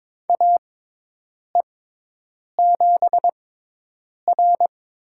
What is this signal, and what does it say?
Morse code, heard as a keyed tone, about 22 wpm, spaced as for 8 wpm, 700 Hz. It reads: AE7R
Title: USA Call Signs - 4 Characters - 6x Character Spacing - 22wpm